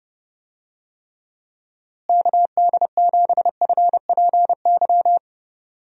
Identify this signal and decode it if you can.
Morse code, heard as a keyed tone, about 30 wpm, 700 Hz. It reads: KD7FPY
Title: USA Call Signs - 6 Characters - Repeated 3x - 30wpm